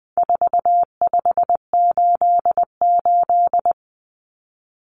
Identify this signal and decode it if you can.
Morse code, heard as a keyed tone, about 20 wpm, 700 Hz. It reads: 4588
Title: Common Cut Numbers - 4-Digit 20wpm